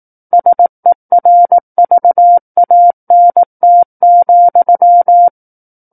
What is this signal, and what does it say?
Morse code, heard as a keyed tone, about 18 wpm, 700 Hz. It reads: SERVANT,
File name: 18 WPM